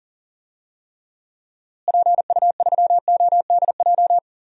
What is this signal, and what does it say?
Morse code, heard as a keyed tone, about 38 wpm, 700 Hz. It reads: PU3ODJ
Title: DXCC-100 Call Signs - 6 Characters - Repeated 3x - 40wpm